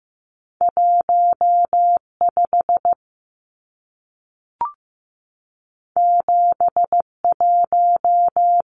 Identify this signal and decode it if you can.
Morse code, heard as a keyed tone, about 15 wpm, 700 Hz. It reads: 15 71
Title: Numbers - 2-Digit 15wpm